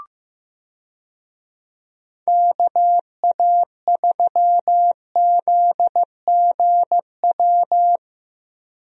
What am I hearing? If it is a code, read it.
Morse code, heard as a keyed tone, about 15 wpm, 700 Hz. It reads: KA3ZGW
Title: USA Call Signs - 6 Characters - Repeated 3x - 15wpm